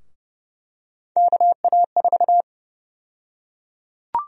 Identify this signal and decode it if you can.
Morse code, heard as a keyed tone, about 30 wpm, 700 Hz. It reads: KA4
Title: USA Call Sign Prefixes - 30wpm